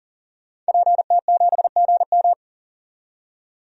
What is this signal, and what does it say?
Morse code, heard as a keyed tone, about 40 wpm, 700 Hz. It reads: PT7GM
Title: DXCC-100 Call Signs - 5 Characters - 40wpm